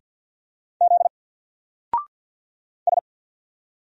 Morse code, heard as a keyed tone, about 50 wpm, 700 Hz. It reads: Z S
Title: Single Character (Letter:Number:Punctuation:Prosign) 50wpm